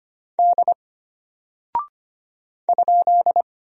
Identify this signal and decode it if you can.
Morse code, heard as a keyed tone, about 25 wpm, 700 Hz. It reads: D ?
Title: Single Character (Letter:Number:Punctuation:Prosign) Alt 25wpm